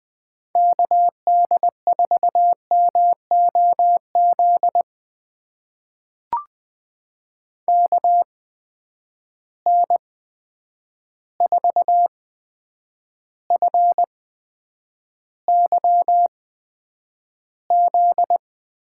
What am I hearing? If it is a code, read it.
Morse code, heard as a keyed tone, about 20 wpm, spaced as for 9 wpm, 700 Hz. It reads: KD4MOZ KN4FYZ
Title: USA Call Signs - 6 Characters - 8x Character Spacing - 20wpm